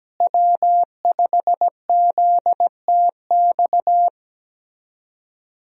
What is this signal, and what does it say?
Morse code, heard as a keyed tone, about 17 wpm, 700 Hz. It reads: W5ZTX
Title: USA Call Signs - 5 Characters - Repeated 2x - 17wpm